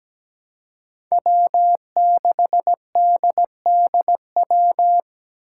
Morse code, heard as a keyed tone, about 17 wpm, 700 Hz. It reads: W6DDW
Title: Five Challenging Letter-Number Combinations 17wpm